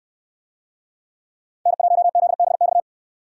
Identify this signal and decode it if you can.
Morse code, heard as a keyed tone, about 64 wpm, 700 Hz. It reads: N1ZBX